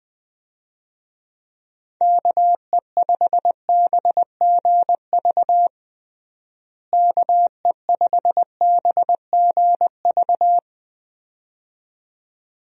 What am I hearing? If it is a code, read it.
Morse code, heard as a keyed tone, about 20 wpm, 700 Hz. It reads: KE5BGV KE5BGV
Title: USA Call Signs - 6 Characters - Repeated 2x - 20wpm